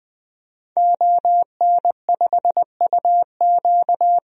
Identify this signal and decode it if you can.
Morse code, heard as a keyed tone, about 20 wpm, 700 Hz. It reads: ON5UQ
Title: DXCC-100 Call Signs - 5 Characters - 20wpm